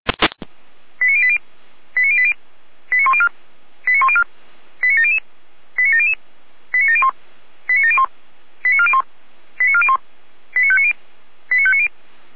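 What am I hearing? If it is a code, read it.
Five-tone selective call (ZVEI), heard as quick runs of five tones, tones of about 70 ms. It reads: 89080 89080 89104 89104 89700 89700 89701 89701 89401 89401 89590 89590
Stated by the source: ZVEI five-tone selective call received on an SDR.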